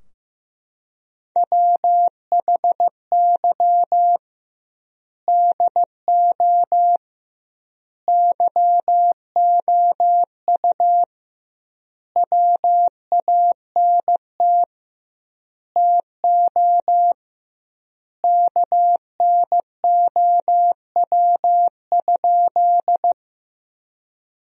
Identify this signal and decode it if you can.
Morse code, heard as a keyed tone, about 15 wpm, 700 Hz. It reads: WHY DO YOU WANT TO KNOW?
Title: 2x Word-Spacing - Sentences from Top 200 Words - 15wpm